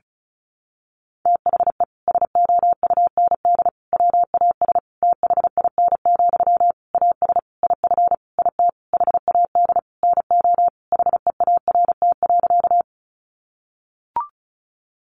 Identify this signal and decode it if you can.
Morse code, heard as a keyed tone, about 34 wpm, 700 Hz. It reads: THE SOUND WAS THIN, AS IF IT HAD NO HEART.
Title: Sentences from Top 1200 Words 35wpm